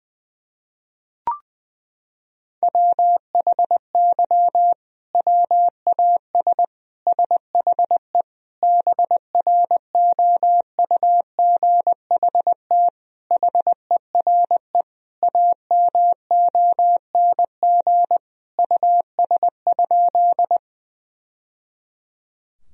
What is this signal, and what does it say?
Morse code, heard as a keyed tone, about 20 wpm, 700 Hz. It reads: WHY WAS SHE BROUGHT HERE AMONG US?
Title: Sentences from Top 1000 Words - Review Part 3 - 20wpm